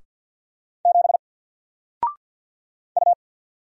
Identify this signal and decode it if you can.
Morse code, heard as a keyed tone, about 48 wpm, 700 Hz. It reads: 7 U
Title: Single Character (Letter:Number:Punctuation:Prosign) Alt 50wpm